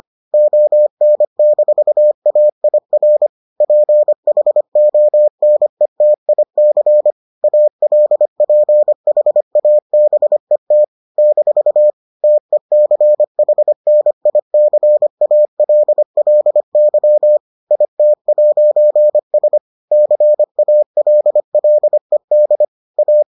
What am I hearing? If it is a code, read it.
Morse code, heard as a keyed tone, about 25 wpm, 600 Hz. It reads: ON-AIR PHONETIC ALPHABET - TECHNICALLY IT'S CALLED A